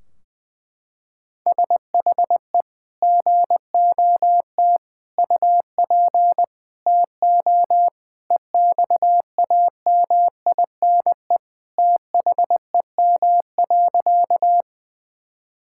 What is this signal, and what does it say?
Morse code, heard as a keyed tone, about 20 wpm, 700 Hz. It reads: SHE GOT UP TO EXAMINE THEM.
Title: Sentences from Top 1600 Words 20wpm